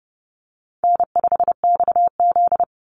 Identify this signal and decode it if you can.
Morse code, heard as a keyed tone, about 31 wpm, 700 Hz. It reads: N5XZ
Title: Easy Call Signs - 30wpm